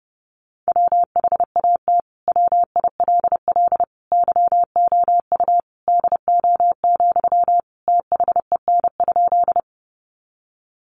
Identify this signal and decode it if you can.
Morse code, heard as a keyed tone, about 30 wpm, 700 Hz. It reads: WHAT WILL YOU DO, THEN?
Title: Sentences from Top 100 Words 30wpm